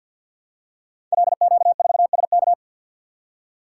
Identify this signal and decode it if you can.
Morse code, heard as a keyed tone, about 50 wpm, 700 Hz. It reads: LQ4SK
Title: Code Groups - 5 Letter:Numbers - 50wpm